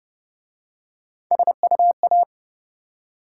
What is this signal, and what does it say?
Morse code, heard as a keyed tone, about 29 wpm, 700 Hz. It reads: SUA